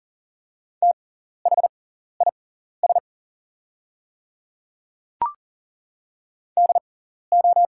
Morse code, heard as a keyed tone, about 40 wpm, spaced as for 14 wpm, 700 Hz. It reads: THIS DO
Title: 6x Character Spacing - Top 100 Words - 40wpm